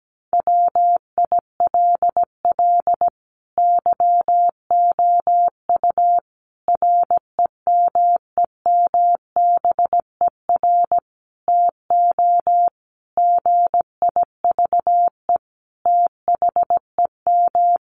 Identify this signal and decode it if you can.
Morse code, heard as a keyed tone, about 17 wpm, 700 Hz. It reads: WILL YOU REMEMBER TO GIVE THEM
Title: Sentences from Top 1100 Words 17wpm